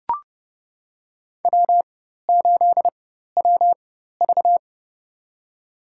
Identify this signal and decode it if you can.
Morse code, heard as a keyed tone, about 30 wpm, spaced as for 14 wpm, 700 Hz. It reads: W8WV
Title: USA Call Signs - 4 Characters - 4x Character Spacing - 30wpm